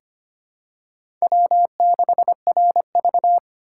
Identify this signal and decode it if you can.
Morse code, heard as a keyed tone, about 25 wpm, 700 Hz. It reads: W6RV
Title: USA Call Signs - 4 Characters - 6x Character Spacing - 25wpm